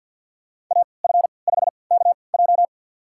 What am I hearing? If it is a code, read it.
Morse code, heard as a keyed tone, about 53 wpm, spaced as for 29 wpm, 700 Hz. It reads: AF5KJ
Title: USA Call Signs - 3x Character Spacing - 50wpm